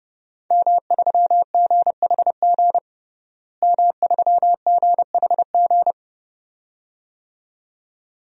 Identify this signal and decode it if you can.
Morse code, heard as a keyed tone, about 30 wpm, 700 Hz. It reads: M3GHG M3GHG
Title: DXCC-100 Call Signs - 5 Characters - Repeated 2x - 30wpm